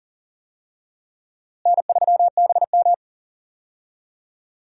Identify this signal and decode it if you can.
Morse code, heard as a keyed tone, about 40 wpm, 700 Hz. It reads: N3BM
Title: USA Call Signs - 4 Characters - Repeated 2x - 40wpm